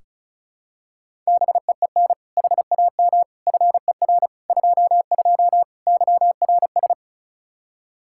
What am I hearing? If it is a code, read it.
Morse code, heard as a keyed tone, about 36 wpm, 700 Hz. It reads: BEEN HAM FER 22 YRS
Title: Practice QSO - Exchange 3 - More Info - 35wpm